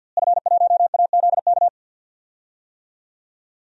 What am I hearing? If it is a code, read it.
Morse code, heard as a keyed tone, about 48 wpm, 700 Hz. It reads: F1AZK
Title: DXCC-100 Call Signs - 5 Characters - Repeated 2x - 50wpm